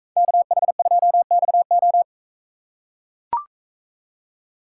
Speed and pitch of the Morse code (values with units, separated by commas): 42 wpm, 700 Hz